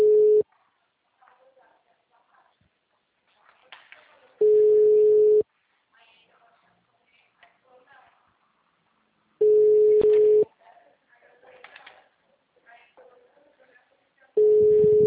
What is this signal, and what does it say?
Ringback tone (Europe, 425 Hz), with speech.